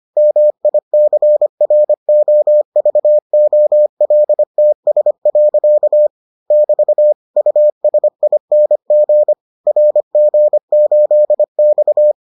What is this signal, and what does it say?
Morse code, heard as a keyed tone, about 25 wpm, 600 Hz. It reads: MICROVOLTS. <BT> USING RG8X